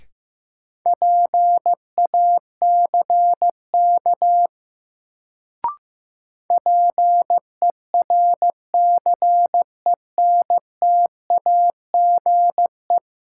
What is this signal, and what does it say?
Morse code, heard as a keyed tone, about 15 wpm, 700 Hz. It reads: PACK PERCENTAGE
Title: Sentences from Top 1300 Words - New Word Drill - 15wpm